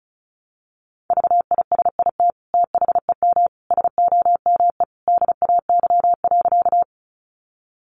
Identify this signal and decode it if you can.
Morse code, heard as a keyed tone, about 34 wpm, 700 Hz. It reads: VISIT THEM SOME DAY.
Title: Sentences from Top 900 Words 35wpm